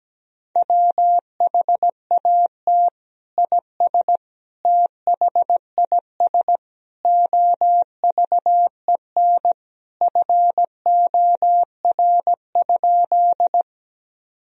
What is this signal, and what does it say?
Morse code, heard as a keyed tone, about 17 wpm, 700 Hz. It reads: WHAT IS THIS OVEN FOR?
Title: Sentences from Top 1000 Words - Review Part 4 - 17wpm